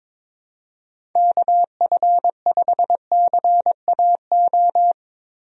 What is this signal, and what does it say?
Morse code, heard as a keyed tone, about 22 wpm, 700 Hz. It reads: KF5CAO
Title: USA Call Signs - 6 Characters - Repeated 3x - 22wpm